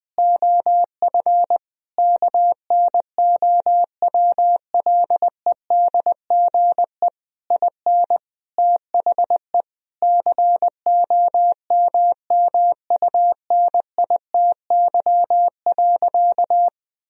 Morse code, heard as a keyed tone, about 21 wpm, 700 Hz. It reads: OF KNOWLEDGE IN THE COMMUNITY.
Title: Sentences from Top 1000 Words - Review Part 4 - 20wpm